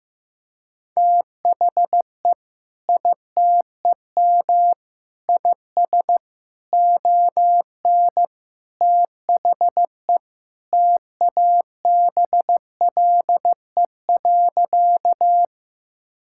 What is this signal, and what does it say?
Morse code, heard as a keyed tone, about 14 wpm, 700 Hz. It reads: THE ITEM IS ON THE TABLE.
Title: Sentences from Top 1000 Words - Review Part 6 - 15wpm